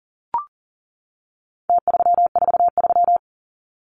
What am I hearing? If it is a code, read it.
Morse code, heard as a keyed tone, about 40 wpm, 700 Hz. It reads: T343
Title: Numbers - Contest Style - 4-Digit 40wpm